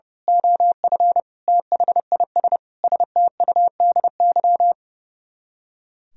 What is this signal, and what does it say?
Morse code, heard as a keyed tone, about 28 wpm, 700 Hz. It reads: OF THIS STUDY
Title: Speed-Racing - Sets of 5 Words 30wpm